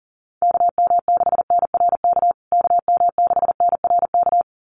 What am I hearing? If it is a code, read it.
Morse code, heard as a keyed tone, about 40 wpm, 700 Hz. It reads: KM6NRK KM6NRK
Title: USA Call Signs Repeated 2x 40wpm